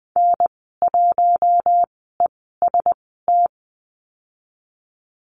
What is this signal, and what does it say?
Morse code, heard as a keyed tone, about 20 wpm, spaced as for 15 wpm, 700 Hz. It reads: N1EST
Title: 042 Lesson 8g - Number 1 - Call Signs 20wpm